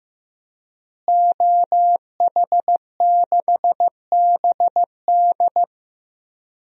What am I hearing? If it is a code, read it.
Morse code, heard as a keyed tone, about 15 wpm, 700 Hz. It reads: OH6BD
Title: DXCC-100 Call Signs - 5 Characters - Repeated 3x - 15wpm